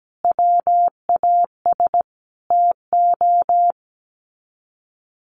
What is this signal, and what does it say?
Morse code, heard as a keyed tone, about 17 wpm, 700 Hz. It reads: WAS TO
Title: Sets of 5 Words 17wpm